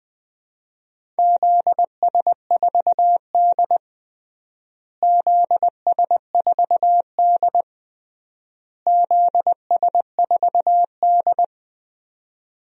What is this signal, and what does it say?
Morse code, heard as a keyed tone, about 20 wpm, 700 Hz. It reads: ZS4D ZS4D ZS4D
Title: DXCC-100 Call Signs - 4 Characters - Repeated 3x - 20wpm